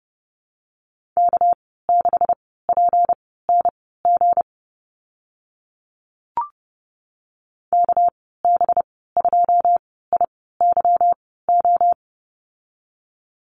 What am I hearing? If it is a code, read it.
Morse code, heard as a keyed tone, about 30 wpm, spaced as for 17 wpm, 700 Hz. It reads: K6PNG KB2IYO
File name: Lesson 37g - Letter x - Call Signs 30wpm